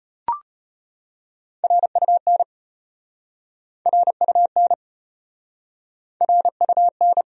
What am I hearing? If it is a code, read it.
Morse code, heard as a keyed tone, about 33 wpm, 700 Hz. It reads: RUN RUN RUN